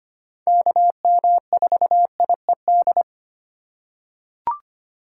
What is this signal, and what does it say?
Morse code, heard as a keyed tone, about 25 wpm, 700 Hz. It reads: KM4IED
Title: USA Call Signs - 6 Characters - 3x Character Spacing - 25wpm